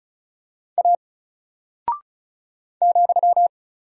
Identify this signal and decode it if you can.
Morse code, heard as a keyed tone, about 36 wpm, 700 Hz. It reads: A ,
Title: Single Character (Letter:Number:Punctuation:Prosign) 35wpm